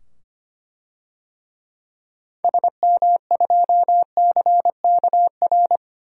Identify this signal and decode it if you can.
Morse code, heard as a keyed tone, about 25 wpm, 700 Hz. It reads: SM2CKR
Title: DXCC-100 Call Signs - 6 Characters - 25wpm